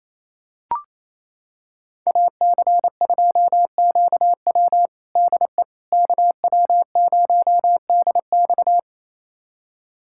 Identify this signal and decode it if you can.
Morse code, heard as a keyed tone, about 28 wpm, 700 Hz. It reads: AC2QW DE KW0DX
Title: USA Callsign DE Callsign - 5 Characters - 28wpm